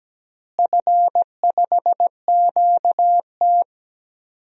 Morse code, heard as a keyed tone, about 17 wpm, 700 Hz. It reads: F5QT
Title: DXCC-100 Call Signs - 4 Characters - Repeated 3x - 17wpm